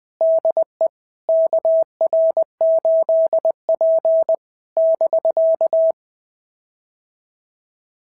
Morse code, heard as a keyed tone, about 20 wpm, 650 Hz. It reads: DE KR8P <BK>
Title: POTA QSO Practice - 20wpm